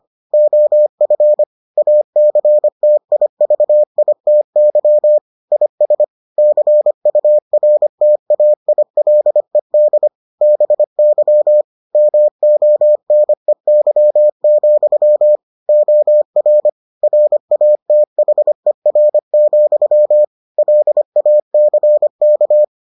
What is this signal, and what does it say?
Morse code, heard as a keyed tone, about 25 wpm, 600 Hz. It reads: OF ACTIVITY IS CURTAILED BY MONEY, OR RATHER, LACK